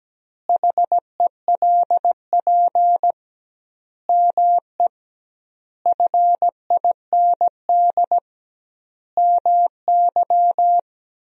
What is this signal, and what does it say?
Morse code, heard as a keyed tone, about 17 wpm, 700 Hz. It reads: HELP ME FIND MY
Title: 2x Word-Spacing - Sentences from Top 200 Words - 17wpm